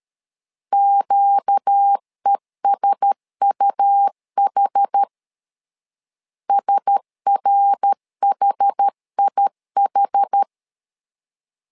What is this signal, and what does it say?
Morse code, heard as a keyed tone, about 13 wpm, 800 Hz. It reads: QESUH SRHIH
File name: LEZIONI / 4 Q1S 12wpm 1802